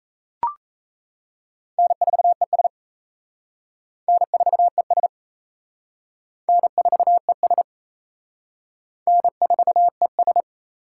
Morse code, heard as a keyed tone, about 34 wpm, 700 Hz. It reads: N4ES N4ES N4ES N4ES
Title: Speed Racing - USA Call Signs 28wpm